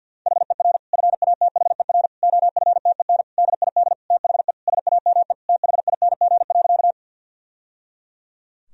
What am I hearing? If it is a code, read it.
Morse code, heard as a keyed tone, about 51 wpm, 700 Hz. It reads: HER FATHER OFTEN DID THE SAME THING.
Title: Sentences from Top 1000 Words - Review Part 1 - 50wpm